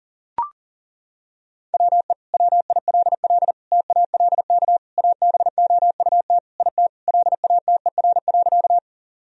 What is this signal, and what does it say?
Morse code, heard as a keyed tone, about 40 wpm, 700 Hz. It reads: WE WILL TALK ABOUT IT LATER.
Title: Sentences from Top 1000 Words - Review Part 4 - 40wpm